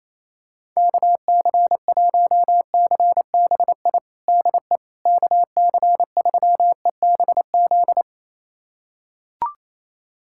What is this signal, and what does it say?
Morse code, heard as a keyed tone, about 27 wpm, 700 Hz. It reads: KC1CBI DE KC3EBZ
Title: USA Callsign DE Callsign - 6 Characters - 28wpm